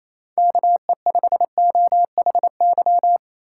Morse code, heard as a keyed tone, about 28 wpm, 700 Hz. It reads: KE5OHY
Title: USA Call Signs - 6 Characters - 3x Character Spacing - 28wpm